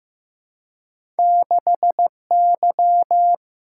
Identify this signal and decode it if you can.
Morse code, heard as a keyed tone, about 15 wpm, 700 Hz. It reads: 6Y
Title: Single-Letter Single-Number 15wpm